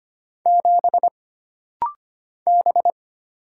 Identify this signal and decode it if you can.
Morse code, heard as a keyed tone, about 25 wpm, 700 Hz. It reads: 7 B
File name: Single Character (Letter:Number:Punctuation:Prosign) - Rapid-Fire - 25wpm